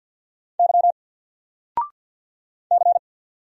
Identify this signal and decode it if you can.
Morse code, heard as a keyed tone, about 50 wpm, 700 Hz. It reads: Y C
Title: Single Letter-Number 50wpm